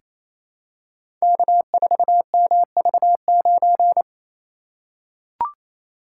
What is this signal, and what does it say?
Morse code, heard as a keyed tone, about 27 wpm, 700 Hz. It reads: K4MV9
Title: Code Groups - 5 Letter:Numbers - 28wpm